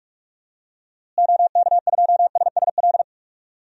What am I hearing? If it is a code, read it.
Morse code, heard as a keyed tone, about 46 wpm, 700 Hz. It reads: OK2SSL
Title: DXCC-100 Call Signs - 6 Characters - Repeated 2x - 45wpm